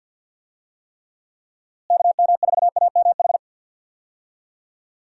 Morse code, heard as a keyed tone, about 51 wpm, 700 Hz. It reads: KM4AMH
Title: USA Call Signs - 6 Characters - Repeated 3x - 50wpm